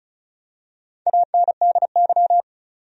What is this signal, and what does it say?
Morse code, heard as a keyed tone, about 34 wpm, 700 Hz. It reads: ANDY
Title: USA Common Names - Max 4 Letters - 35wpm